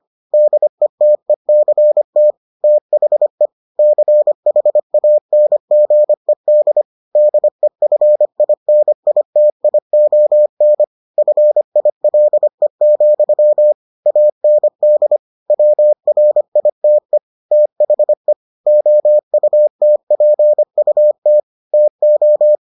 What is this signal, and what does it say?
Morse code, heard as a keyed tone, about 25 wpm, 600 Hz. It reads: DETECT THE CHANGED DEFINITION FILE, AND WRITE THE OUTPUT TO